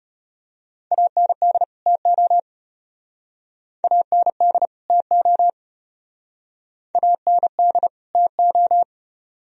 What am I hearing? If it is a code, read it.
Morse code, heard as a keyed tone, about 32 wpm, 700 Hz. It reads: AND TO AND TO AND TO